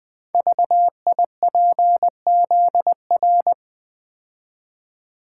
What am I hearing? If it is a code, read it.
Morse code, heard as a keyed tone, about 20 wpm, 700 Hz. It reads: VIPZR
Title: Code Groups - 5 Letters - 20wpm